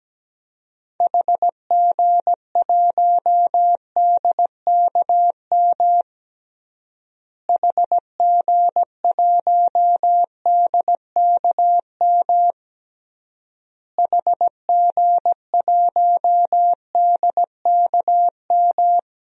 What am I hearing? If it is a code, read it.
Morse code, heard as a keyed tone, about 17 wpm, 700 Hz. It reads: HG1DKM HG1DKM HG1DKM
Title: DXCC-100 Call Signs - 6 Characters - Repeated 3x - 17wpm